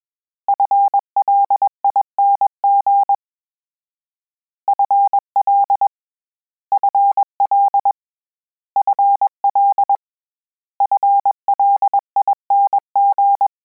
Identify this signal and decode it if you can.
Morse code, heard as a keyed tone, about 21 wpm, 800 Hz. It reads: FLING FL FL FL FLING
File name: Combinations / Combinations-mp3s / Combinations@20